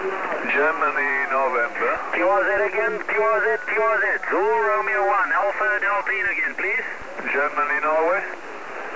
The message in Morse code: none